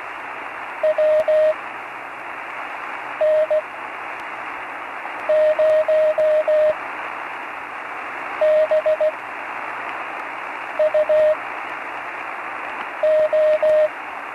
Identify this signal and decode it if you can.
Morse code, heard as a keyed tone, about 17 wpm, spaced as for 5 wpm, 630 Hz, with radio noise. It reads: WN0BUO
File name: CW Learning Programs / Learning Morse Code (from Tune in the World) / 35 answers